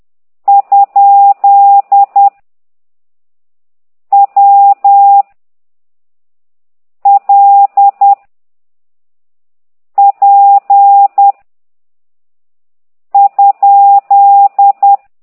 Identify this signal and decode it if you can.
Morse code, heard as a keyed tone, about 10 wpm, 800 Hz. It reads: ? W L P ?
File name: ct2idw / exerc9